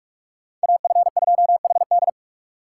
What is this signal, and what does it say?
Morse code, heard as a keyed tone, about 40 wpm, 700 Hz. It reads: AF2HD